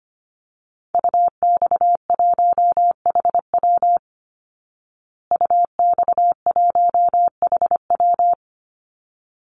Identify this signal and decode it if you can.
Morse code, heard as a keyed tone, about 25 wpm, 700 Hz. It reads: UX1HW UX1HW
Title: Call Signs Repeated 3x - 25wpm